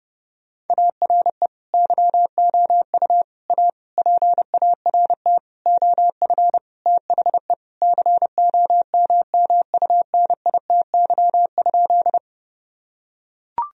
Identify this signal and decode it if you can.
Morse code, heard as a keyed tone, about 30 wpm, 700 Hz. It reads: ARE YOU A PART OF THE COMMUNITY?
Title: Sentences from Top 1000 Words - Review Part 4 - 30wpm